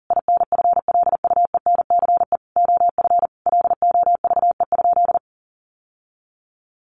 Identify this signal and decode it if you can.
Morse code, heard as a keyed tone, about 40 wpm, 700 Hz. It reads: INFLUENCE OF LOVE?
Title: Sentences from Top 1100 Words 40wpm